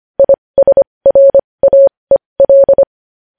Morse code, heard as a keyed tone, about 25 wpm, spaced as for 20 wpm, 550 Hz. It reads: ISRAEL
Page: News Headlines in Morse Code at 20 WPM